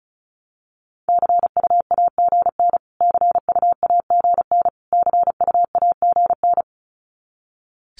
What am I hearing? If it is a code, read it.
Morse code, heard as a keyed tone, about 34 wpm, 700 Hz. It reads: CUAGN CUAGN CUAGN
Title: Abbreviations Repeated 3x - 35wpm